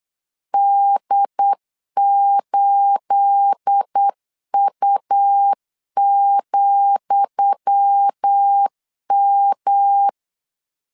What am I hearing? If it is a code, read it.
Morse code, heard as a keyed tone, about 8 wpm, 800 Hz. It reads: D8U,M
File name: LEZIONI / 10 SIGNS / 10D COMA 8wpm 3749